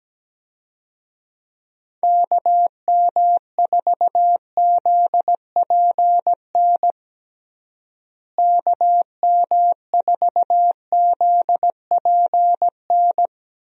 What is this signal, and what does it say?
Morse code, heard as a keyed tone, about 17 wpm, 700 Hz. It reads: KM4ZPN KM4ZPN
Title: USA Call Signs - 6 Characters - Repeated 2x - 17wpm